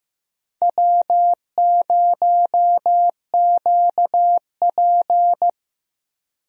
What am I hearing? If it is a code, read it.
Morse code, heard as a keyed tone, about 15 wpm, 700 Hz. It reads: W0QP
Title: USA Call Signs - 4 Characters - 15wpm